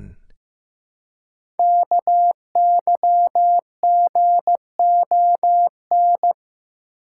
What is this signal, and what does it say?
Morse code, heard as a keyed tone, about 15 wpm, 700 Hz. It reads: KYGON
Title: Code Groups - 5 Letters - 15wpm